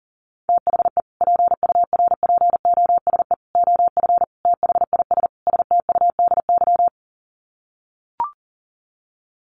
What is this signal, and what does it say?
Morse code, heard as a keyed tone, about 37 wpm, 700 Hz. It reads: THE PURPOSE OF THIS STUDY
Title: Sets of 5 Words 40wpm